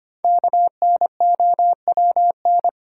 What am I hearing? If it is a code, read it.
Morse code, heard as a keyed tone, about 25 wpm, 700 Hz. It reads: KNOWN
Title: Sentences from Top 600 Words - New Word Drill - 25wpm